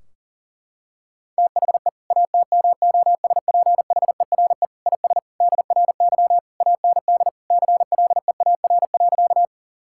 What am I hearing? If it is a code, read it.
Morse code, heard as a keyed tone, about 40 wpm, 700 Hz. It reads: THE ATMOSPHERE IS DRY AND CLEAR.
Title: Sentences from Top 1600 Words 40wpm